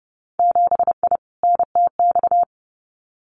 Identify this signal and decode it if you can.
Morse code, heard as a keyed tone, about 30 wpm, 700 Hz. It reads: 7I NTX
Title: Winter Field Day Exchanges Encore 30wpm